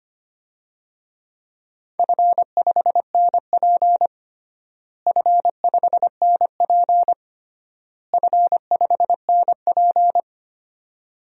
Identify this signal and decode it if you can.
Morse code, heard as a keyed tone, about 25 wpm, 700 Hz. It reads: F5NP F5NP F5NP
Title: DXCC-100 Call Signs - 4 Characters - Repeated 3x - 25wpm